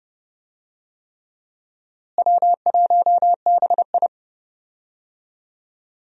Morse code, heard as a keyed tone, about 31 wpm, 700 Hz. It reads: W1BI
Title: USA Call Signs - 4 Characters - Repeated 3x - 30wpm